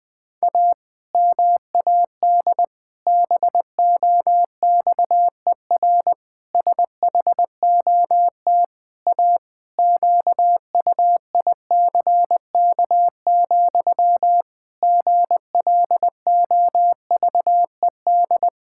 Morse code, heard as a keyed tone, about 20 wpm, 700 Hz. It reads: A MAD BOXER SHOT A QUICK, GLOVED
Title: Pangram Challenge v2 - 20wpm